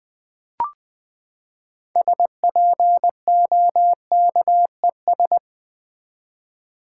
Morse code, heard as a keyed tone, about 20 wpm, 700 Hz. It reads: SPOKES
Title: Build A Word - 6 Letters At A Time - 12-Letter Words - 20wpm